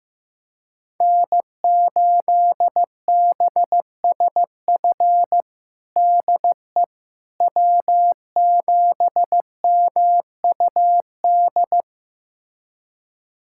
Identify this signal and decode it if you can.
Morse code, heard as a keyed tone, about 15 wpm, 700 Hz. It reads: N8BSF DE W7MUD
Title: USA Callsign DE Callsign - 5 Characters - 15wpm